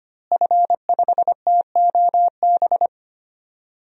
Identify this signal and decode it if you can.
Morse code, heard as a keyed tone, about 25 wpm, 700 Hz. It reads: F5TOB
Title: DXCC-100 Call Signs - 5 Characters - Repeated 3x - 25wpm